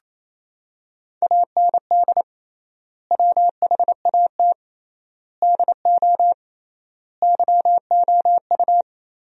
Morse code, heard as a keyed tone, about 28 wpm, 700 Hz. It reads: AND WHAT DO YOU
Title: 3x Word-Spacing - Sentences from Top 100 Words - 28wpm